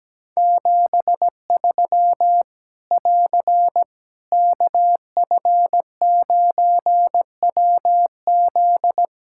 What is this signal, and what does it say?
Morse code, heard as a keyed tone, about 17 wpm, 700 Hz. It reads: 73 <AR> KF9WZ
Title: Practice QSO - Exchange 4 - Ending - 17wpm